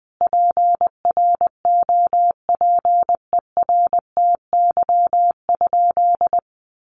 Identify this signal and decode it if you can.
Morse code, heard as a keyed tone, about 20 wpm, 700 Hz. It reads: PROPERTY?
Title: Sentences from Top 1000 Words 20wpm